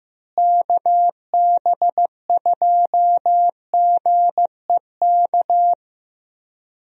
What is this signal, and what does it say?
Morse code, heard as a keyed tone, about 15 wpm, 700 Hz. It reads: KB2GEK
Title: USA Call Signs - 6 Characters - Repeated 2x - 15wpm